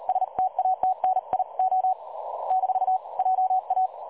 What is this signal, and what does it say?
Morse code, heard as a keyed tone, about 38 wpm, 750 Hz. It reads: STATNIO - JA
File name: dlazaawansowanych / nagraniazpasma / websdr recording 3565.0kHz